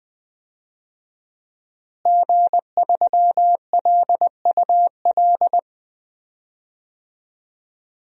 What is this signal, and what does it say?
Morse code, heard as a keyed tone, about 20 wpm, 700 Hz. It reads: G3LUL
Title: DXCC-100 Call Signs - 5 Characters - Repeated 3x - 20wpm